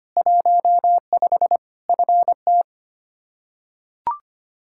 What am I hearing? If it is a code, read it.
Morse code, heard as a keyed tone, about 25 wpm, 700 Hz. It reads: 15 FT
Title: K1USN SST QSO Practice 25wpm